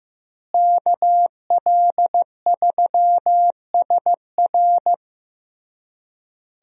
Morse code, heard as a keyed tone, about 15 wpm, 700 Hz. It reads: KL3SR
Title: USA Call Signs - 5 Characters - Repeated 3x - 15wpm